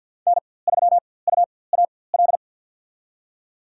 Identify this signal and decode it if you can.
Morse code, heard as a keyed tone, about 52 wpm, spaced as for 23 wpm, 700 Hz. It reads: N3UAL
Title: USA Call Signs - 5 Characters - 4x Character Spacing - 50wpm